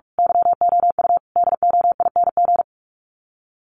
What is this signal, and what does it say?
Morse code, heard as a keyed tone, about 44 wpm, 700 Hz. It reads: YOU DOING
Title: Sets of 4 Words 45wpm